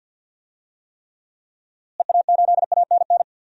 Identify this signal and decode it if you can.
Morse code, heard as a keyed tone, about 50 wpm, 700 Hz. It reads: EA8ANN
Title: DXCC-100 Call Signs - 6 Characters - Repeated 2x - 50wpm